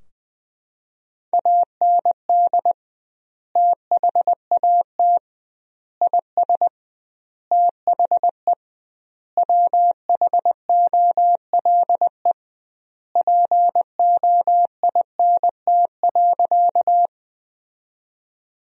Morse code, heard as a keyed tone, about 20 wpm, 700 Hz. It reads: AND THAT IS THE WHOLE POINT.
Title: 2x Word-Spacing - Sentences from Top 400 Words - 20wpm